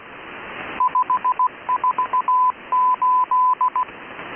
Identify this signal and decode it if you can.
Morse code, heard as a keyed tone, about 16 wpm, 1010 Hz, with radio noise. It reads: 548
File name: SharedFiles / AudTfer / M12 18576 CW 0020z